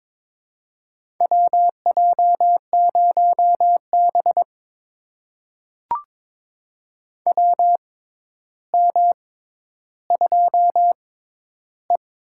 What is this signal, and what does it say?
Morse code, heard as a keyed tone, about 22 wpm, spaced as for 11 wpm, 700 Hz. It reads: WJ0B WM2E
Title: USA Call Signs - 4 Characters - 6x Character Spacing - 22wpm